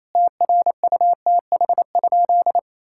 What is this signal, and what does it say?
Morse code, heard as a keyed tone, about 27 wpm, 700 Hz. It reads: TRUTH?
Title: Sentences from Top 1600 Words 28wpm